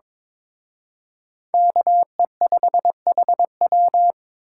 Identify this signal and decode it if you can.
Morse code, heard as a keyed tone, about 26 wpm, 700 Hz. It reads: KE5HW
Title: Speed Racing - USA 2x2 Call Signs 22wpm